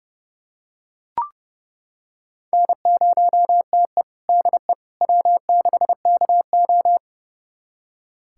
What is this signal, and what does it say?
Morse code, heard as a keyed tone, about 32 wpm, 700 Hz. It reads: N0TE DE W6KO
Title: USA Callsign DE Callsign - 4 Characters - 30wpm